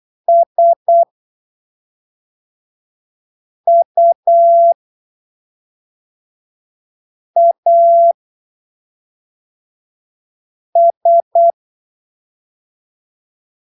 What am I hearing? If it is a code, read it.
Morse code, heard as a keyed tone, about 8 wpm, spaced as for 3 wpm, 680 Hz. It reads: SUAS
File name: arquivos / texto / texto 23 08ppm